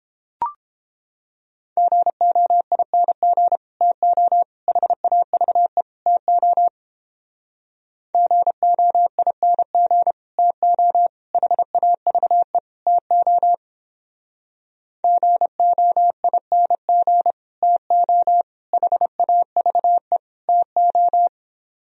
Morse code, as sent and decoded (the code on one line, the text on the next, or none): --. --- .. -. --. / - --- / .... .- ...- . / - --- / --. --- .. -. --. / - --- / .... .- ...- . / - --- / --. --- .. -. --. / - --- / .... .- ...- . / - ---
GOING TO HAVE TO GOING TO HAVE TO GOING TO HAVE TO